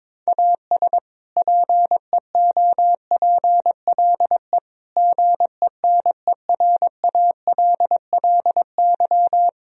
Morse code, heard as a keyed tone, about 22 wpm, 700 Hz. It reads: AS PEOPLE GENERALLY
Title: Sentences from Top 1000 Words - Review Part 4 - 22wpm